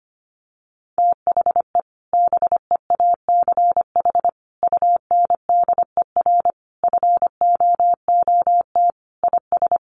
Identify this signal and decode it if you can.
Morse code, heard as a keyed tone, about 25 wpm, 700 Hz. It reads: THE BEACH UNDER FOOT IS
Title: Sentences from Top 1500 Words 25wpm